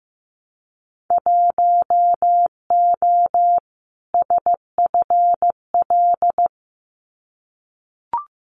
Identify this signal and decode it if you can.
Morse code, heard as a keyed tone, about 15 wpm, 700 Hz. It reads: 1O SFL
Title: Winter Field Day Exchanges 15wpm